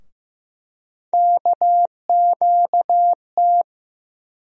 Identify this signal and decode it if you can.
Morse code, heard as a keyed tone, about 15 wpm, 700 Hz. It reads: KQT
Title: USA Call Sign Postfixes - 3 Letters 15wpm